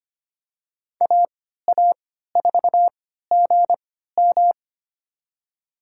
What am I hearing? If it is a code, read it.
Morse code, heard as a keyed tone, about 25 wpm, spaced as for 14 wpm, 700 Hz. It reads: AA4GM
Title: USA Call Signs - 5 Characters - 3x Character Spacing - 25wpm